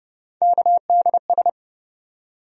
Morse code, heard as a keyed tone, about 30 wpm, 700 Hz. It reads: KDS
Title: USA Call Sign Postfixes - 3 Letters 30wpm